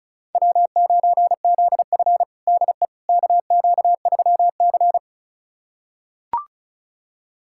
Morse code, heard as a keyed tone, about 34 wpm, 700 Hz. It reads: W9ZF DE KQ3C
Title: USA Callsign DE Callsign - 4 Characters - 35wpm